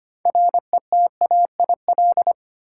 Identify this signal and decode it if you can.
Morse code, heard as a keyed tone, about 25 wpm, 700 Hz. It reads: RETAIL
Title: Practice QSO - Exchange 3 - More Info - 25wpm